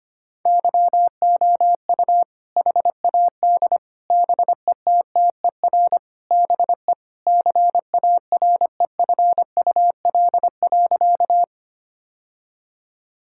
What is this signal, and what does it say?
Morse code, heard as a keyed tone, about 25 wpm, 700 Hz. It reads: YOU HAD BETTER BE CAREFUL.
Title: Sentences from Top 1600 Words 25wpm